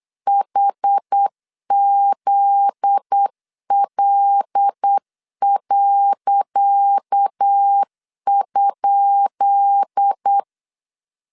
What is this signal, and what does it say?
Morse code, heard as a keyed tone, about 8 wpm, 800 Hz. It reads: HZL.?
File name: LEZIONI / 10 SIGNS / 10C DOT 8wpm 3649